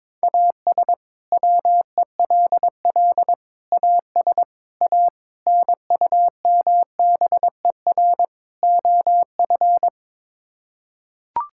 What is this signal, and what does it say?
Morse code, heard as a keyed tone, about 22 wpm, 700 Hz. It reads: AS WELL AS A NUMBER OF
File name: practice / 22wpm / Sets of 6 Words - 22wpm